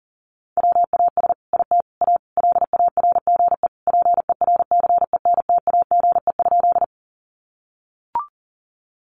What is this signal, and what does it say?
Morse code, heard as a keyed tone, about 37 wpm, 700 Hz. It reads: WAS IT A LARGE PERCENTAGE?
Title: Sentences from Top 1300 Words 40wpm